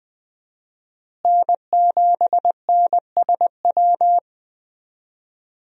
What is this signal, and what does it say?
Morse code, heard as a keyed tone, about 20 wpm, 700 Hz. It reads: N7NSW
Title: USA Call Signs - 5 Characters - Repeated 2x - 20wpm